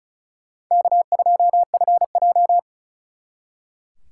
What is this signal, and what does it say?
Morse code, heard as a keyed tone, about 45 wpm, 700 Hz. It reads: K2FJ